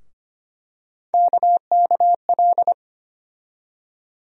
Morse code, heard as a keyed tone, about 25 wpm, 700 Hz. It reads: KKL
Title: USA Call Sign Postfixes - 3 Letters 25wpm